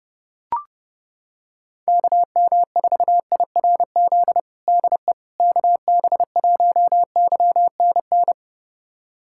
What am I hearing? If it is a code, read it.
Morse code, heard as a keyed tone, about 30 wpm, 700 Hz. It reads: KM4IRZ DE KB1YNN
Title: USA Callsign DE Callsign - 6 Characters - 30wpm